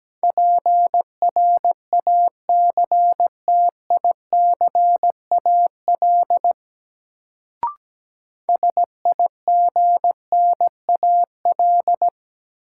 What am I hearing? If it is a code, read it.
Morse code, heard as a keyed tone, about 17 wpm, 700 Hz. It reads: PRACTICAL SIGNAL
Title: Sentences from Top 1400 Words - New Word Drill Alt - 17wpm